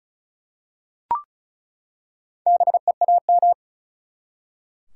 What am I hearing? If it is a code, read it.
Morse code, heard as a keyed tone, about 36 wpm, 700 Hz. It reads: BEAM
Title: Top 180 QSO Elements 35wpm